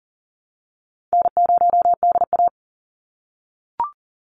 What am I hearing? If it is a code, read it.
Morse code, heard as a keyed tone, about 40 wpm, 700 Hz. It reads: N0DA
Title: NA SOTA Chaser Callsigns 40wpm